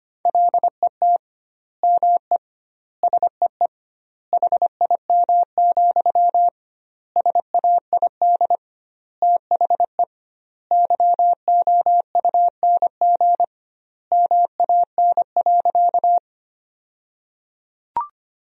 Morse code, heard as a keyed tone, about 25 wpm, 700 Hz. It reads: LET ME SEE HIM, SAID THE YOUNG MAN.
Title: 2x Word-Spacing - Sentences from Top 400 Words - 25wpm